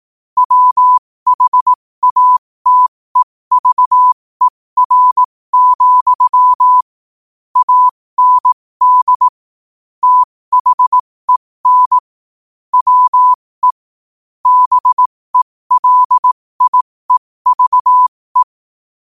Morse code, heard as a keyed tone, about 18 wpm, spaced as for 15 wpm, 1000 Hz. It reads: WHATEVER, AND THEN WE BELIEVE
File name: qotdfiles / 15wpm